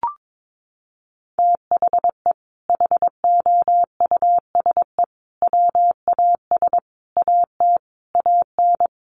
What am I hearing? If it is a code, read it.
Morse code, heard as a keyed tone, about 22 wpm, 700 Hz. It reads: THE HOUSE WAS AT AN
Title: Sentences from Top 1500 Words 22wpm